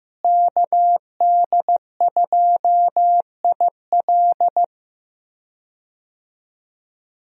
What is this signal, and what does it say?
Morse code, heard as a keyed tone, about 15 wpm, 700 Hz. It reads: KD2IL
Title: USA Call Signs - 5 Characters - Repeated 2x - 15wpm